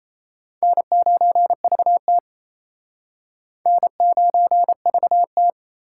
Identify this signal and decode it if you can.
Morse code, heard as a keyed tone, about 34 wpm, 700 Hz. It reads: N9VT N9VT
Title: Speed Racing - USA 1x2 Call Signs 28wpm